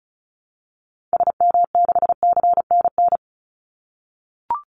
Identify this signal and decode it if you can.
Morse code, heard as a keyed tone, about 34 wpm, 700 Hz. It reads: SM6CNN
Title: Call Signs - 35wpm